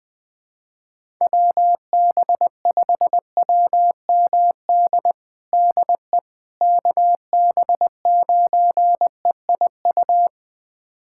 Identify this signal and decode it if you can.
Morse code, heard as a keyed tone, about 21 wpm, 700 Hz. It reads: WB5WMD DE KB9EIU
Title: USA Callsign DE Callsign - 6 Characters - 20wpm